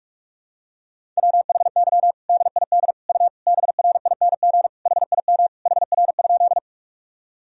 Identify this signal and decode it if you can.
Morse code, heard as a keyed tone, about 41 wpm, 700 Hz. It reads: WHY DID U BRING HIM HR?
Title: Sentences from Top 500 Words and Abbreviations 45wpm